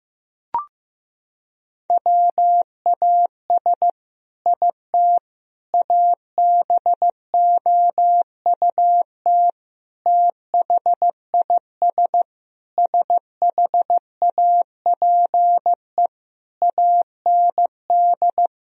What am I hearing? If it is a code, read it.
Morse code, heard as a keyed tone, about 16 wpm, 700 Hz. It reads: WAS IT ABOUT THIS SHAPE AND
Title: Sentences from Top 1000 Words - Review Part 3 - 15wpm